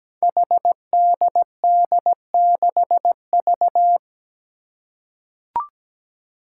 Morse code, heard as a keyed tone, about 17 wpm, 700 Hz. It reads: HDD6V
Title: Five Challenging Letter-Number Combinations 17wpm